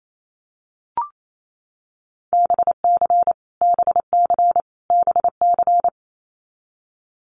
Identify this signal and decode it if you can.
Morse code, heard as a keyed tone, about 28 wpm, 700 Hz. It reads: BC BC BC
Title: Canadian Provinces and Territories - Repeated 3x - 28wpm